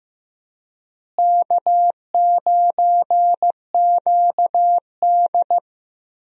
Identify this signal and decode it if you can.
Morse code, heard as a keyed tone, about 15 wpm, 700 Hz. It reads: K9QD
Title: USA Call Signs - 4 Characters - Repeated 2x - 15wpm